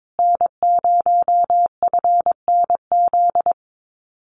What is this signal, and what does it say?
Morse code, heard as a keyed tone, about 22 wpm, 700 Hz. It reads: N0FNZ
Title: Common QSO Phrases Encore 22wpm